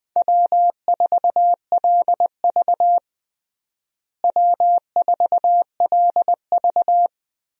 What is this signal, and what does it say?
Morse code, heard as a keyed tone, about 20 wpm, 700 Hz. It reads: W4LV W4LV
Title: USA Call Signs - 4 Characters - Repeated 2x - 20wpm